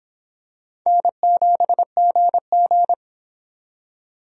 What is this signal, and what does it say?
Morse code, heard as a keyed tone, about 26 wpm, 700 Hz. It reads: N7GG